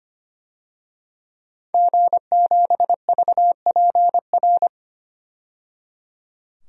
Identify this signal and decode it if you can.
Morse code, heard as a keyed tone, about 25 wpm, 700 Hz. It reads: G7VPR
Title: DXCC-100 Call Signs - 5 Characters - 25wpm